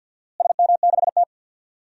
Morse code, heard as a keyed tone, about 52 wpm, 700 Hz. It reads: SM6T
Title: DXCC-100 Call Signs - 4 Characters - Repeated 3x - 50wpm